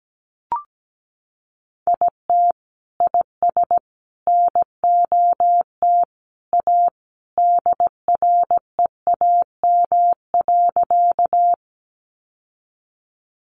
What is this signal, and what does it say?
Morse code, heard as a keyed tone, about 17 wpm, 700 Hz. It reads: IT IS NOT A DREAM.
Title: Sentences from Top 1500 Words 17wpm